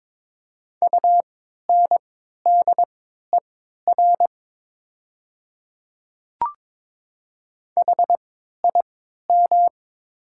Morse code, heard as a keyed tone, about 22 wpm, spaced as for 13 wpm, 700 Hz. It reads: UNDER HIM
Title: 3x Character Spacing - Top 200 Words - 22wpm